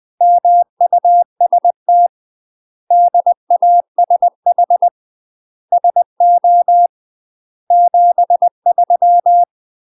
Morse code, heard as a keyed tone, about 20 wpm, 700 Hz. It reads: MUST DASH SO 73
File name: 20WPM